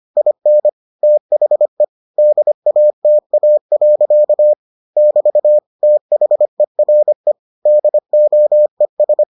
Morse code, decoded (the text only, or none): IN THE DATA. <BT> THERE DOES